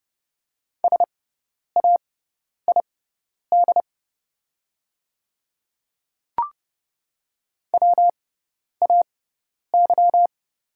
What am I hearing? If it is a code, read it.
Morse code, heard as a keyed tone, about 30 wpm, spaced as for 10 wpm, 700 Hz. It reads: SAID WAY
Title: 6x Character Spacing - Top 100 Words - 30wpm